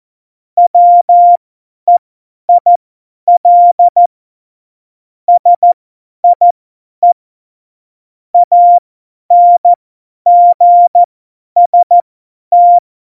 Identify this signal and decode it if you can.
Morse code, heard as a keyed tone, about 14 wpm, spaced as for 10 wpm, 700 Hz. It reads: WEIL SIE ANGST
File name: Morsekurs / Witze / Witz 07. November 2017, 50 F70